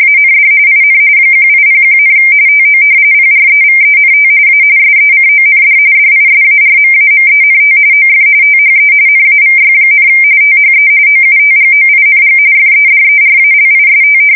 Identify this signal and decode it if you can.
RTTY, about 45.45 baud. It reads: AND IN 1801, TO PREVENT A SECOND DISARMAMENT OF THE NAVY, THE OUTGOING FEDERALIST
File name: fsk170 / usnavy1